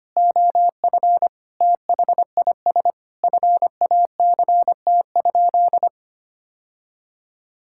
Morse code, heard as a keyed tone, about 25 wpm, 700 Hz. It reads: OF THIS FACT?
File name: practice / 25wpm / Sentences with 2-Word Phrases - 25wpm